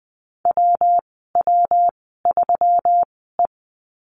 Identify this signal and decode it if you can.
Morse code, heard as a keyed tone, about 20 wpm, spaced as for 15 wpm, 700 Hz. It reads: WW3E
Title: 147 Lesson 24g - Letter y - Call Signs 20wpm